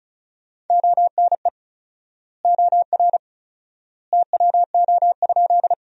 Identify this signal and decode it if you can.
Morse code, heard as a keyed tone, about 34 wpm, 700 Hz. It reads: ONE OR TWO?
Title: 4x Word-Spacing - Sentences from Top 100 Words - 35wpm